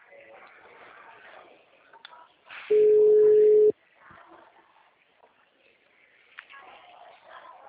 Ringback tone (Europe, 425 Hz).